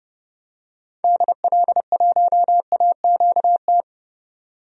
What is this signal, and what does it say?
Morse code, heard as a keyed tone, about 30 wpm, 700 Hz. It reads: DL1AQT
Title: DXCC-100 Call Signs - 6 Characters - 30wpm